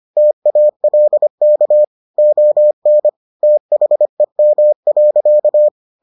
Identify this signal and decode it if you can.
Morse code, heard as a keyed tone, about 25 wpm, 600 Hz. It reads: TALK ON THEM.